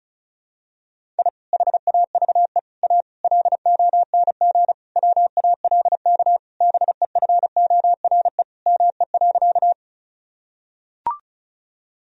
Morse code, heard as a keyed tone, about 36 wpm, 700 Hz. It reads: I HAVE A LONG WALK BEFORE ME.
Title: Sentences from Top 1000 Words - Review Part 1 - 35wpm